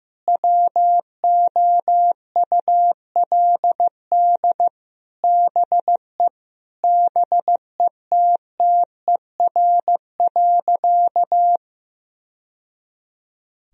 Morse code, heard as a keyed tone, about 15 wpm, 700 Hz. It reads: WOULD BE BETTER.
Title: Sentences from Top 1000 Words - Review Part 1 - 15wpm